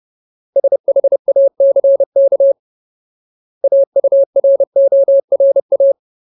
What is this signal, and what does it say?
Morse code, heard as a keyed tone, about 30 wpm, 550 Hz. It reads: SHACK AURORA
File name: Ultimate Word List / 30WPM Ultimate